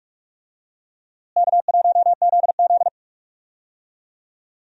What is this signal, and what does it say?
Morse code, heard as a keyed tone, about 45 wpm, 700 Hz. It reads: K1ZZ
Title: USA Call Signs - 4 Characters - Repeated 3x - 45wpm